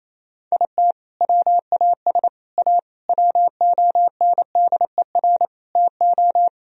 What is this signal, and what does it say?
Morse code, heard as a keyed tone, about 28 wpm, 700 Hz. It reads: IT WAS A WONDER TO
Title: Sentences from Top 1000 Words - Review Part 3 - 28wpm